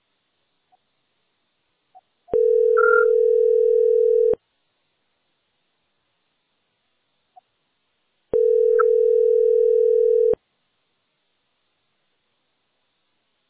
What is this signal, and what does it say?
Ringback tone (North America), 440 + 480 Hz.